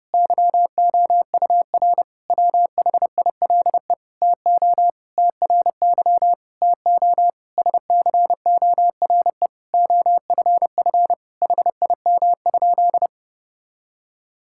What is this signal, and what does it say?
Morse code, heard as a keyed tone, about 30 wpm, 700 Hz. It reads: YOUR WHILE TO TRY TO SCORE OFF HIM?
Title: Sentences from Top 1600 Words 30wpm